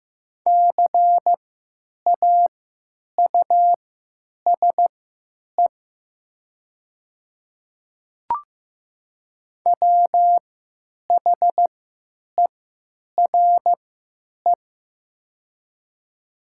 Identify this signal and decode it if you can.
Morse code, heard as a keyed tone, about 15 wpm, spaced as for 9 wpm, 700 Hz. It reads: CAUSE WHERE
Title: 3x Character Spacing - Top 300 Words - 15wpm